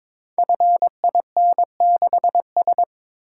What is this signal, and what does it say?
Morse code, heard as a keyed tone, about 22 wpm, 700 Hz. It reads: FIN6S
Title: Code Groups - 5 Letter:Numbers - 22wpm